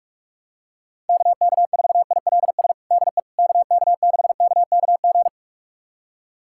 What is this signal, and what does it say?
Morse code, heard as a keyed tone, about 42 wpm, 700 Hz. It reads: KK4ILS DE KK6KKG
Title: USA Callsign DE Callsign - 6 Characters - 45wpm